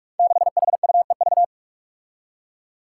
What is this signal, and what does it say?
Morse code, heard as a keyed tone, about 46 wpm, 700 Hz. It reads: 6HUEV
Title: Five Challenging Letter-Number Combinations 45wpm